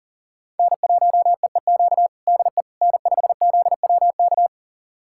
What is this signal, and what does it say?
Morse code, heard as a keyed tone, about 40 wpm, 700 Hz. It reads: N1EEQ DE N5ZWK
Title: USA Callsign DE Callsign - 5 Characters - 40wpm